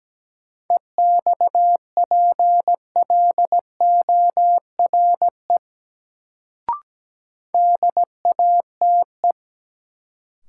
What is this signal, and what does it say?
Morse code, heard as a keyed tone, about 17 wpm, 700 Hz. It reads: EXPLORE DATE
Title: Sentences from Top 1100 Words - New Word Drill Alt - 17wpm